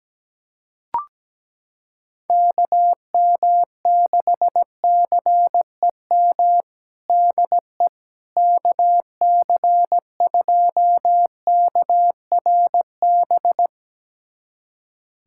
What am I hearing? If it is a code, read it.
Morse code, heard as a keyed tone, about 17 wpm, 700 Hz. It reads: KM6CEM DE KC2KRB
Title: USA Callsign DE Callsign - 6 Characters - 17wpm